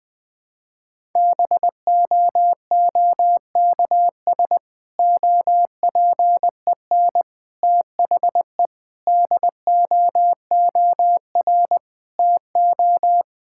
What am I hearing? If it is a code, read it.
Morse code, heard as a keyed tone, about 20 wpm, 700 Hz. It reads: BOOKS OPEN THE DOOR TO
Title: Difficult Sentences with Hints - 20wpm